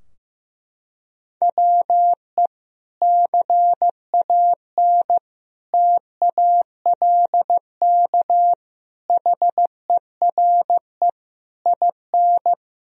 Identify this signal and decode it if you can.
Morse code, heard as a keyed tone, about 14 wpm, 700 Hz. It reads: WE CAN TALK HERE IN
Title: Sentences from Top 1000 Words - Review Part 4 - 15wpm